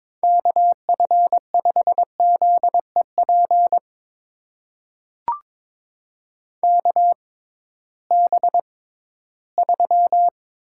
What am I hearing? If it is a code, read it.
Morse code, heard as a keyed tone, about 22 wpm, spaced as for 14 wpm, 700 Hz. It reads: KF5ZEP KB3
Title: USA Call Signs - 6 Characters - 6x Character Spacing - 22wpm